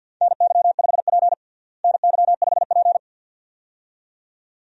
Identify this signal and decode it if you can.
Morse code, heard as a keyed tone, about 47 wpm, 700 Hz. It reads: NY5P NY5P
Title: USA Call Signs - 4 Characters - Repeated 2x - 50wpm